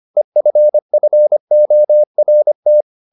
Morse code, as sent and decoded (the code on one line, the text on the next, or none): . ..-. ..-. --- .-. -
EFFORT